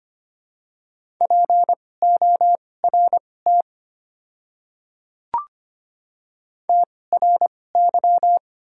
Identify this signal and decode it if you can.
Morse code, heard as a keyed tone, about 25 wpm, spaced as for 18 wpm, 700 Hz. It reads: PORT TRY
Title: 2x Character Spacing - Top 300 Words - 25wpm